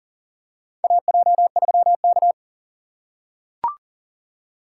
Morse code, heard as a keyed tone, about 40 wpm, 700 Hz. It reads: AJ3K
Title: USA Call Signs - 4 Characters - 2x Character Spacing - 40wpm